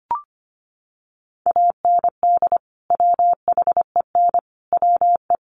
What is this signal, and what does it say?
Morse code, heard as a keyed tone, about 25 wpm, 700 Hz. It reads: AND WHEN WE
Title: Sets of 5 Words 25wpm